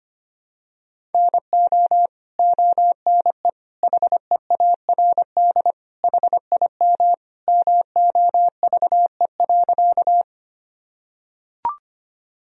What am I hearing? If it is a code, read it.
Morse code, heard as a keyed tone, about 25 wpm, 700 Hz. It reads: NO ONE HEARD HIM MOVE.
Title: Sentences from Top 1000 Words - Review Part 1 - 25wpm